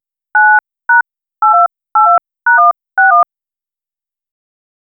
DTMF, heard as telephone touch tones, tones of about 150 ms. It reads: 9#8282#161
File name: phone-dialling--jq4plsyc